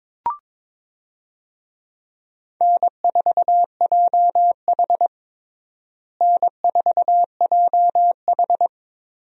Morse code, heard as a keyed tone, about 22 wpm, 700 Hz. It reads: N4JH N4JH
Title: USA Call Signs - 4 Characters - Repeated 3x - 22wpm